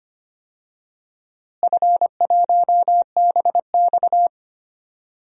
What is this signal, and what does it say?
Morse code, heard as a keyed tone, about 25 wpm, 700 Hz. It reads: F1BX
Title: DXCC-100 Call Signs - 4 Characters - Repeated 2x - 25wpm